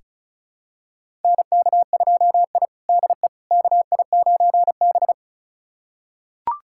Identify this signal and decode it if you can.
Morse code, heard as a keyed tone, about 36 wpm, 700 Hz. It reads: NK2I DE KI9B
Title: USA Callsign DE Callsign - 4 Characters - 35wpm